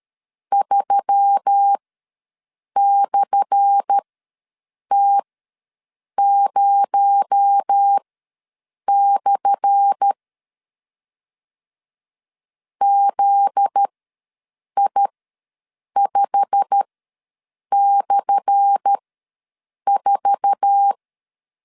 Morse code, heard as a keyed tone, about 13 wpm, spaced as for 7 wpm, 800 Hz. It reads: 3/T0/ ZI5/4
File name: LEZIONI / 10 SIGNS / 10A SLASH ws 12wpm 3443